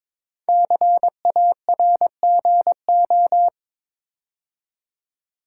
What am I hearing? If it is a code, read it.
Morse code, heard as a keyed tone, about 22 wpm, 700 Hz. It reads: CARGO
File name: 5 Letter Words - First 3 Letters Given - 22wpm